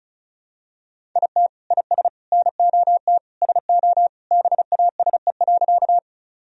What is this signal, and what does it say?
Morse code, heard as a keyed tone, about 34 wpm, 700 Hz. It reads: IT IS NOT SO BASE.
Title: Sentences from Top 1000 Words - Review Part 1 - 35wpm